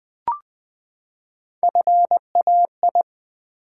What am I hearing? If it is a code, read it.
Morse code, heard as a keyed tone, about 20 wpm, 700 Hz. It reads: FAI
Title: Build A Word - 3 Letters At A Time - 8-Letter Words - 20wpm